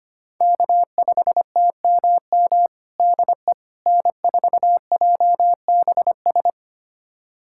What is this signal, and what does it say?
Morse code, heard as a keyed tone, about 25 wpm, 700 Hz. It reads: K5TMM DE N4JBS
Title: USA Callsign DE Callsign - 5 Characters - 25wpm